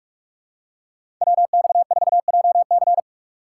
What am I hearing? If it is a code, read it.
Morse code, heard as a keyed tone, about 42 wpm, 700 Hz. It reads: WX4JC
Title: USA Call Signs - 5 Characters - 6x Character Spacing - 45wpm